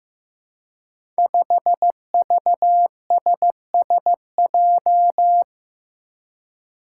Morse code, heard as a keyed tone, about 15 wpm, 700 Hz. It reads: 5VSSJ
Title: Five Challenging Letter-Number Combinations 15wpm